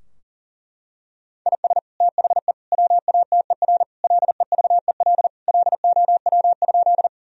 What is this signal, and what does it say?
Morse code, heard as a keyed tone, about 40 wpm, 700 Hz. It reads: IS THE WATER LEVEL LOW?
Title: Sentences from Top 1000 Words - Review Part 4 - 40wpm